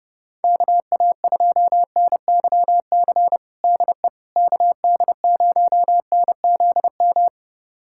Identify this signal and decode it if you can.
Morse code, heard as a keyed tone, about 32 wpm, 700 Hz. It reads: KA2NYC DE KD0NZM
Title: USA Callsign DE Callsign - 6 Characters - 30wpm